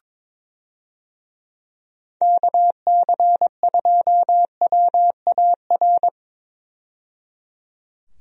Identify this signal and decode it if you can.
Morse code, heard as a keyed tone, about 21 wpm, 700 Hz. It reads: KC2WAR
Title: USA Call Signs - 6 Characters - 22wpm